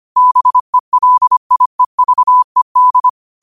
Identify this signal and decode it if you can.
Morse code, heard as a keyed tone, about 25 wpm, 1000 Hz. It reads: BELIEVED
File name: 25wpm